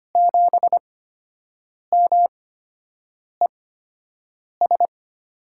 Morse code, heard as a keyed tone, about 25 wpm, spaced as for 7 wpm, 700 Hz. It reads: 7MES
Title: USA Call Signs - 8x Character Spacing - 25wpm